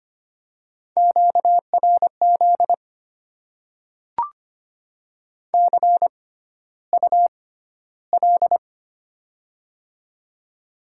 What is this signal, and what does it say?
Morse code, heard as a keyed tone, about 25 wpm, spaced as for 13 wpm, 700 Hz. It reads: QRZ CUL
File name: Top 90 QSO Elements - 6x Character Spacing - 25wpm